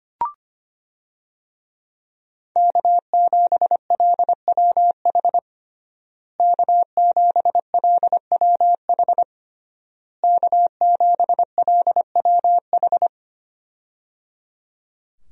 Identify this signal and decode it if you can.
Morse code, heard as a keyed tone, about 25 wpm, 700 Hz. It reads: K7LWH K7LWH K7LWH
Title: USA Call Signs - 5 Characters - Repeated 3x - 25wpm